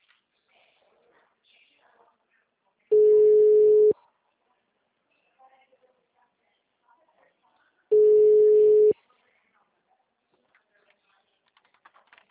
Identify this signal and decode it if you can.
Ringback tone (Europe, 425 Hz), with speech.